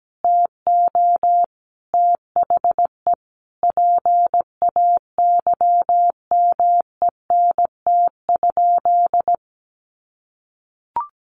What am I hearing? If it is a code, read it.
Morse code, heard as a keyed tone, about 17 wpm, 700 Hz. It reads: TO THE PAYMENT?
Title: Sentences from Top 1400 Words 17wpm